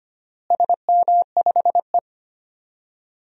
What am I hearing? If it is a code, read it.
Morse code, heard as a keyed tone, about 25 wpm, 700 Hz. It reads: SM5E
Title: DXCC-100 Call Signs - 4 Characters - Repeated 2x - 25wpm